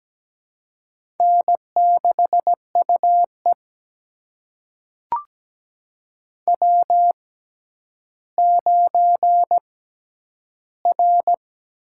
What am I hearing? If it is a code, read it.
Morse code, heard as a keyed tone, about 17 wpm, spaced as for 10 wpm, 700 Hz. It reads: N6UE W9R
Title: USA Call Signs - 6x Character Spacing - 17wpm